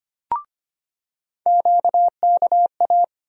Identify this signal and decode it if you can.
Morse code, heard as a keyed tone, about 25 wpm, 700 Hz. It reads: QKA
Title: USA Call Sign Postfixes - 3 Letters 25wpm